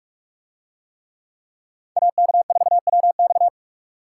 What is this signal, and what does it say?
Morse code, heard as a keyed tone, about 46 wpm, 700 Hz. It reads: AK4WX